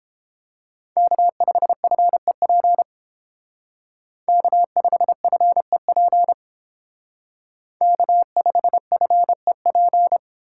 Morse code, heard as a keyed tone, about 28 wpm, 700 Hz. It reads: K5FEP K5FEP K5FEP